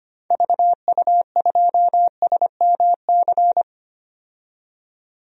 Morse code, heard as a keyed tone, about 25 wpm, 700 Hz. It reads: VU2SMC
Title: DXCC-100 Call Signs - 6 Characters - 25wpm